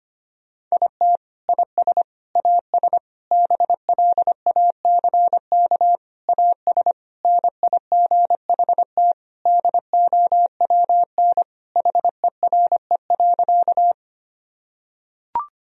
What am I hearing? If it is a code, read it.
Morse code, heard as a keyed tone, about 25 wpm, 700 Hz. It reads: IT IS AS BLACK AS NIGHT DOWN HERE.
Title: Sentences from Top 1000 Words - Review Part 2 - 25wpm